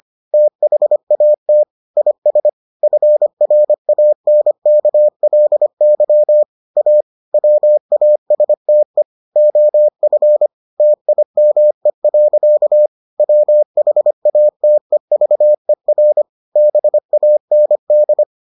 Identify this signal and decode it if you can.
Morse code, heard as a keyed tone, about 25 wpm, 600 Hz. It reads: THAT IS FRANKLY A WASTE OF TIME. WHATEVER BAND